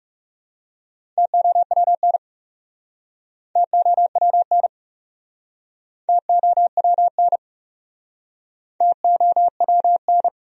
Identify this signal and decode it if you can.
Morse code, heard as a keyed tone, about 43 wpm, 700 Hz. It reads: TOWN TOWN TOWN TOWN